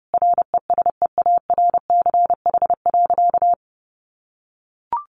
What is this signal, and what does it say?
Morse code, heard as a keyed tone, about 30 wpm, 700 Hz. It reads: RESEARCH.
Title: Sentences from Top 1400 Words 30wpm